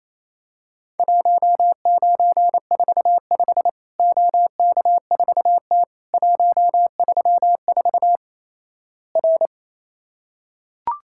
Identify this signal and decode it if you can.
Morse code, heard as a keyed tone, about 28 wpm, 700 Hz. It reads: 1945 OK4T 134 R
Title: Worked All Europe DX Contest - QTC Exchange - 28wpm